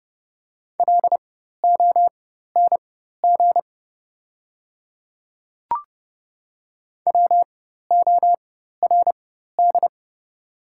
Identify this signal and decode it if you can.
Morse code, heard as a keyed tone, about 30 wpm, spaced as for 14 wpm, 700 Hz. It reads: LONG WORD
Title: 4x Character Spacing - Top 200 Words - 30wpm